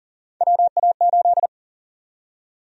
Morse code, heard as a keyed tone, about 40 wpm, 700 Hz. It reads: WA8
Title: USA Call Sign Prefixes - Repeated 2x - 40wpm